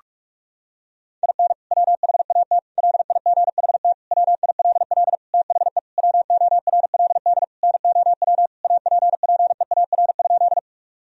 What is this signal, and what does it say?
Morse code, heard as a keyed tone, about 46 wpm, 700 Hz. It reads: IN WHAT LIGHT WILL THE WORLD NOW APPEAR?
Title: Sentences from Top 1000 Words - Review Part 2 - 45wpm